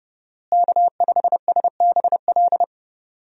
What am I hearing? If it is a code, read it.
Morse code, heard as a keyed tone, about 30 wpm, 700 Hz. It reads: K5SBL
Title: USA Call Signs - 5 Characters - Repeated 2x - 30wpm